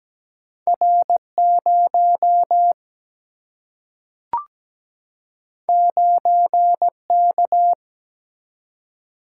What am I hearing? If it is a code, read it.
Morse code, heard as a keyed tone, about 17 wpm, 700 Hz. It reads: R 0 9 K
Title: Single-Letter Single-Number 17wpm